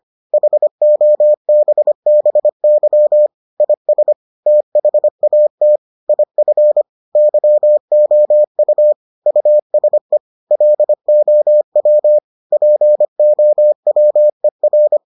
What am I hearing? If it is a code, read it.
Morse code, heard as a keyed tone, about 25 wpm, 600 Hz. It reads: HOBBY IS THAT IF YOU USE LOW POWER